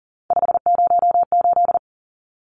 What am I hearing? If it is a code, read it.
Morse code, heard as a keyed tone, about 40 wpm, 700 Hz. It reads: 508
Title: Numbers - 3-Digit 40wpm